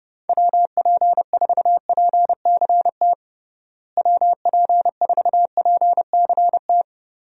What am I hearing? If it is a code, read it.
Morse code, heard as a keyed tone, about 32 wpm, 700 Hz. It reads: WP4PCT WP4PCT
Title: USA Call Signs - 6 Characters - Repeated 3x - 30wpm